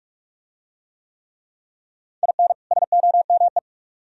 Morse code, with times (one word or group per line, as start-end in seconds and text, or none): 2.2-2.5 IN
2.7-3.6 SOME